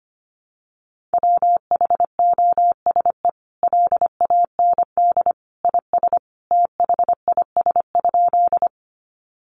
Morse code, heard as a keyed tone, about 25 wpm, 700 Hz. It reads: WHOSE LAND IS THIS?
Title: Sentences from Top 1200 Words 25wpm